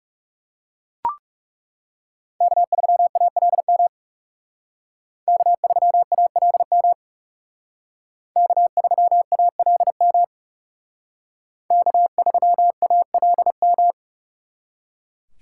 Morse code, heard as a keyed tone, about 39 wpm, 700 Hz. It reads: K3ALM K3ALM K3ALM K3ALM